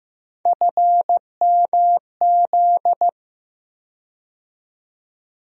Morse code, heard as a keyed tone, about 15 wpm, 700 Hz. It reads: FMZ
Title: USA Call Sign Postfixes - 3 Letters 15wpm